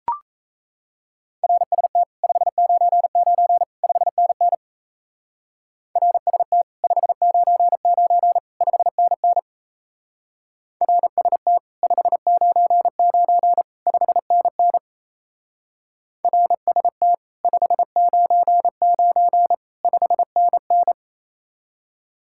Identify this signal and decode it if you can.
Morse code, heard as a keyed tone, about 32 wpm, 700 Hz. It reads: RST 599 5NN RST 599 5NN RST 599 5NN RST 599 5NN
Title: Speed-Racing Common QSO Phrases - 28wpm